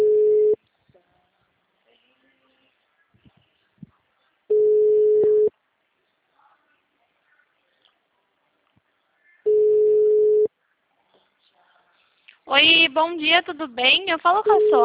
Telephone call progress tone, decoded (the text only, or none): ringback tone